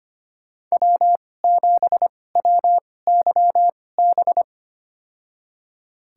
Morse code, heard as a keyed tone, about 25 wpm, spaced as for 18 wpm, 700 Hz. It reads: W7WYB
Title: USA Call Signs - 5 Characters - 2x Character Spacing - 25wpm